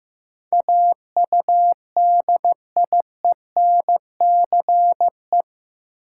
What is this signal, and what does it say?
Morse code, heard as a keyed tone, about 15 wpm, 700 Hz. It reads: AUDIENCE
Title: Sentences from Top 800 Words - New Word Drill - 15wpm